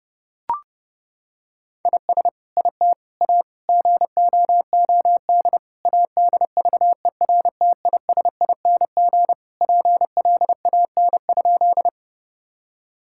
Morse code, heard as a keyed tone, about 32 wpm, 700 Hz. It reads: IS IT A GOOD ADVERTISING PLAN?
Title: Sentences from Top 1000 Words - Review Part 6 - 30wpm